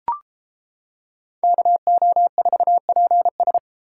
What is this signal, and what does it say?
Morse code, heard as a keyed tone, about 29 wpm, 700 Hz. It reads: KO4PS